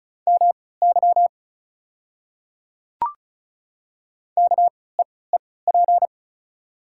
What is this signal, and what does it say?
Morse code, heard as a keyed tone, about 34 wpm, spaced as for 20 wpm, 690 Hz. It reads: MY KEEP
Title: 3x Character Spacing - Top 500 Words - 35wpm